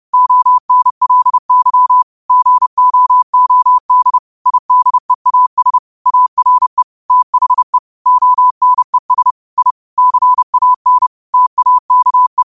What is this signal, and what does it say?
Morse code, heard as a keyed tone, about 30 wpm, 1000 Hz. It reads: ONLY GOOD IDEAS ARE THE ONES I CAN TAKE
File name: qotdfiles / 30wpm